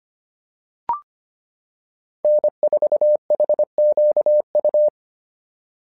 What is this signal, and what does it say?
Morse code, heard as a keyed tone, about 25 wpm, 600 Hz. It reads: N4HQU
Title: SOTA QSO Practice 25wpm